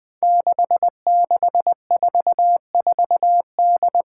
Morse code, heard as a keyed tone, about 20 wpm, 700 Hz. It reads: 6644D
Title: Five Challenging Letter-Number Combinations 20wpm